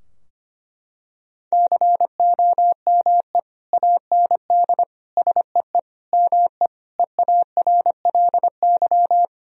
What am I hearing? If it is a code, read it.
Morse code, heard as a keyed tone, about 25 wpm, 700 Hz. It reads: COME AND SEE ME EARLY
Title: Sentences from Top 1000 Words - Review Part 2 - 25wpm